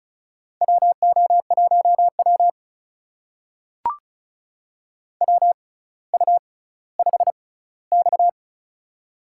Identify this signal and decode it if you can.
Morse code, heard as a keyed tone, about 36 wpm, spaced as for 18 wpm, 700 Hz. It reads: WO1W WU5X
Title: USA Call Signs - 4 Characters - 6x Character Spacing - 35wpm